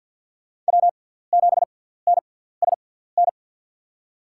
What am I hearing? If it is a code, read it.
Morse code, heard as a keyed tone, about 53 wpm, spaced as for 17 wpm, 700 Hz. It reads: W7NSN
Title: USA Call Signs - 6x Character Spacing - 50wpm